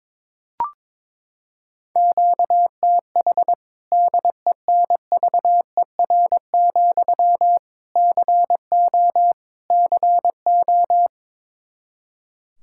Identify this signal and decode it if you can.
Morse code, heard as a keyed tone, about 22 wpm, 700 Hz. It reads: QTH DENVER, CO CO
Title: K1USN SST QSO Practice 22wpm